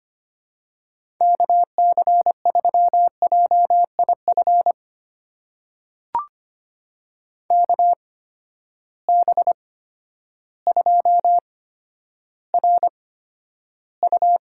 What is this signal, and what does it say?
Morse code, heard as a keyed tone, about 25 wpm, spaced as for 11 wpm, 700 Hz. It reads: KC3JIF KB2RU
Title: USA Call Signs - 6 Characters - 8x Character Spacing - 25wpm